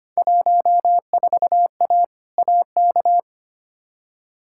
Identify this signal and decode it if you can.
Morse code, heard as a keyed tone, about 25 wpm, 700 Hz. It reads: 14A AK
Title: Field Day Exchanges Encore - 25wpm